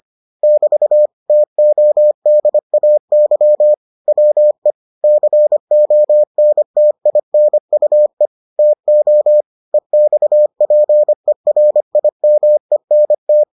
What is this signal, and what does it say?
Morse code, heard as a keyed tone, about 25 wpm, 600 Hz. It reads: <BT> TODAY WE CONTINUE TO EXPERIMENT